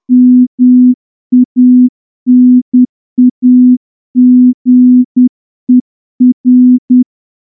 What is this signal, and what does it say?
Morse code, heard as a keyed tone, about 10 wpm, 250 Hz. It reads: MANAGER
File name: morse-training-10WPM